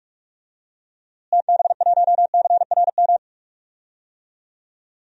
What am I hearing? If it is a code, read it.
Morse code, heard as a keyed tone, about 42 wpm, 700 Hz. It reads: TB1CRM
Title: DXCC-100 Call Signs - 6 Characters - Repeated 3x - 45wpm